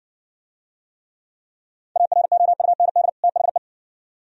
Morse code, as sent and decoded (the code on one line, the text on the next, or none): .- .-. --- ..- -. -.. / - .... .
AROUND THE